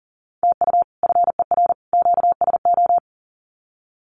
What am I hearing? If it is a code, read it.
Morse code, heard as a keyed tone, about 40 wpm, 700 Hz. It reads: TU FER QSO
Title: Common QSO Phrases 40wpm